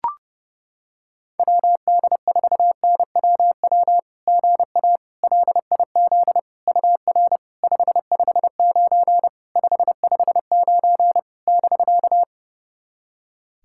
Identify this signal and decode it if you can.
Morse code, heard as a keyed tone, about 30 wpm, 700 Hz. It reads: WD4NWW GA LIZ UR 559 559 <BK>
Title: SOTA QSO Practice 30wpm